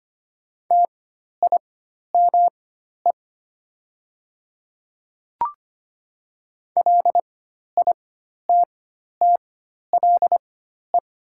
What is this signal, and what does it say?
Morse code, heard as a keyed tone, about 25 wpm, spaced as for 12 wpm, 700 Hz. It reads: TIME LITTLE
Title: 4x Character Spacing - Top 200 Words - 25wpm